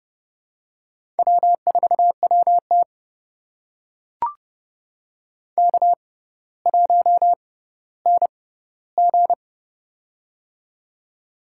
Morse code, heard as a keyed tone, about 30 wpm, spaced as for 15 wpm, 700 Hz. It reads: W4WT K1NG
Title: USA Call Signs - 4 Characters - 6x Character Spacing - 30wpm